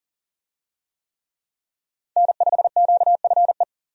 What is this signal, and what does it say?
Morse code, heard as a keyed tone, about 37 wpm, 700 Hz. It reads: N5QFE
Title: USA Call Signs - 5 Characters - Repeated 3x - 40wpm